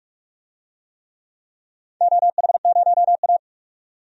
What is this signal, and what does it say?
Morse code, heard as a keyed tone, about 44 wpm, 700 Hz. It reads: OH0A
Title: DXCC-100 Call Signs - 4 Characters - 45wpm